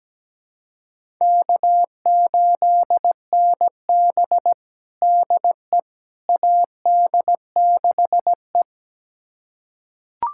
Morse code, heard as a keyed tone, about 17 wpm, 700 Hz. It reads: K8NB DE AD6E
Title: USA Callsign DE Callsign - 4 Characters - 17wpm